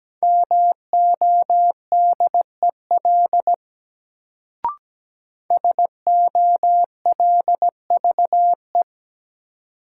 Morse code, heard as a keyed tone, about 17 wpm, 700 Hz. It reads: MODEL SOLVE
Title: Sentences from Top 1000 Words - New Word Drill - 17wpm